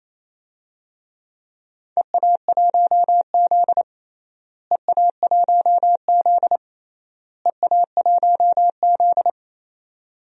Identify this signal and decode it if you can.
Morse code, heard as a keyed tone, about 28 wpm, 700 Hz. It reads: EA1Z EA1Z EA1Z
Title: DXCC-100 Call Signs - 4 Characters - Repeated 3x - 28wpm